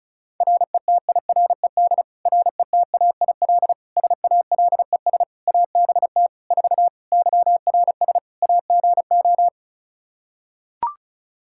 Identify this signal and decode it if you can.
Morse code, heard as a keyed tone, about 34 wpm, 700 Hz. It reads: RETIRED RETAIL SALES ABT 4 YRS AGO
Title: Practice QSO - Exchange 3 - More Info - 35wpm